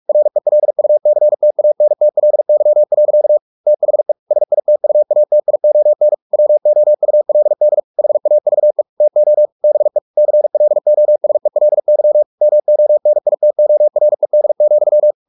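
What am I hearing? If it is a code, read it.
Morse code, heard as a keyed tone, about 44 wpm, 600 Hz. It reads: RELUCTANTLY. THE SITUATION WOULD HAVE TO BE CLOSELY MONITORED,
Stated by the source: Morse practice at 45 wpm, the novel Never by Ken Follett in English, from CW med Gnister.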